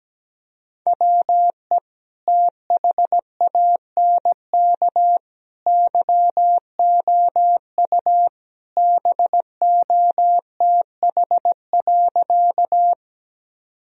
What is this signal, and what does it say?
Morse code, heard as a keyed tone, about 17 wpm, 700 Hz. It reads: WE THANK YOU BOTH.
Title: Sentences from Top 1000 Words - Review Part 1 - 17wpm